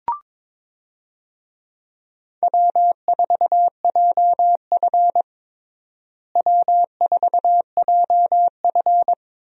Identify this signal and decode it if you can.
Morse code, heard as a keyed tone, about 22 wpm, 700 Hz. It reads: W4JF W4JF
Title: USA Call Signs - 4 Characters - Repeated 3x - 22wpm